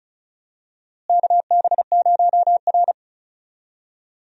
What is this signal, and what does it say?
Morse code, heard as a keyed tone, about 36 wpm, 700 Hz. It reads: KB0R
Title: USA Call Signs - 4 Characters - Repeated 3x - 35wpm